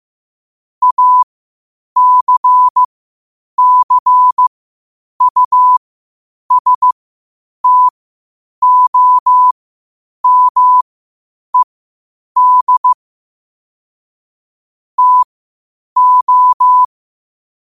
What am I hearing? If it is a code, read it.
Morse code, heard as a keyed tone, about 15 wpm, spaced as for 8 wpm, 1000 Hz. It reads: ACCUSTOMED TO
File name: qotdfiles / 8wpm